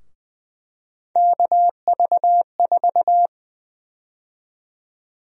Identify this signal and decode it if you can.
Morse code, heard as a keyed tone, about 20 wpm, 700 Hz. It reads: KV4
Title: USA Call Sign Prefixes - 20wpm